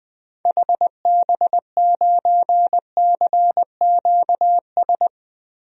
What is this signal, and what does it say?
Morse code, heard as a keyed tone, about 20 wpm, 700 Hz. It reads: HB9CQS
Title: DXCC-100 Call Signs - 6 Characters - 20wpm